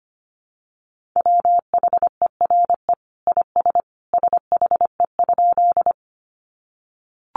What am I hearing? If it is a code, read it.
Morse code, heard as a keyed tone, about 25 wpm, 700 Hz. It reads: WHERE IS SHE?
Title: Sentences from Top 200 Words 25wpm